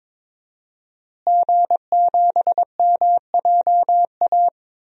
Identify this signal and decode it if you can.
Morse code, heard as a keyed tone, about 22 wpm, 700 Hz. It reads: G7MJA
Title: DXCC-100 Call Signs - 5 Characters - Repeated 2x - 22wpm